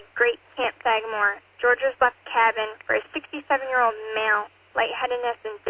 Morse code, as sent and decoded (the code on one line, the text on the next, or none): none